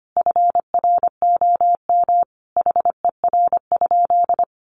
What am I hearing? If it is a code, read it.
Morse code, heard as a keyed tone, about 25 wpm, 700 Hz. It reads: FROM HER?
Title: Sentences from Top 900 Words 25wpm